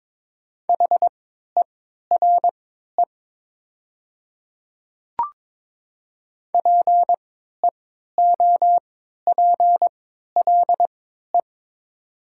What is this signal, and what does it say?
Morse code, heard as a keyed tone, about 22 wpm, spaced as for 12 wpm, 700 Hz. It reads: HERE PEOPLE
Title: 3x Character Spacing - Top 200 Words - 22wpm